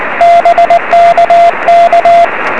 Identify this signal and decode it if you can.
Morse code, heard as a keyed tone, about 19 wpm, 690 Hz, with radio noise. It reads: BKK
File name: 0814z dl3jan 23cm